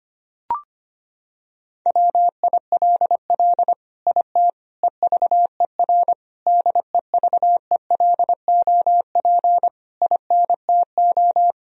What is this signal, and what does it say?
Morse code, heard as a keyed tone, about 25 wpm, 700 Hz. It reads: WILL IT EVER DEVELOP INTO
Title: Sentences from Top 1000 Words - Review Part 2 - 25wpm